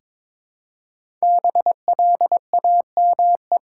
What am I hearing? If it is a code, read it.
Morse code, heard as a keyed tone, about 22 wpm, 700 Hz. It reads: BLAME
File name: 5 Letter Words - First 3 Letters Given - 22wpm